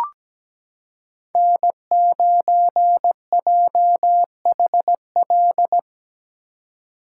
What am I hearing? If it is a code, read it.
Morse code, heard as a keyed tone, about 17 wpm, 700 Hz. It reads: N9JHL
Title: K1USN SST QSO Practice - 17wpm